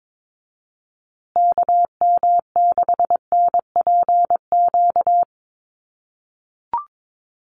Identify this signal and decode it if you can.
Morse code, heard as a keyed tone, about 22 wpm, 700 Hz. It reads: KM6NPQ
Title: USA Call Signs Repeated 3x 22wpm